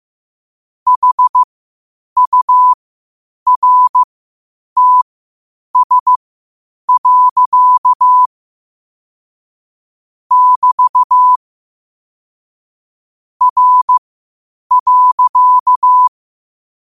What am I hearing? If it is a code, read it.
Morse code, heard as a keyed tone, about 15 wpm, spaced as for 8 wpm, 1000 Hz. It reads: HURTS. <BT> R.
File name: qotdfiles / 8wpm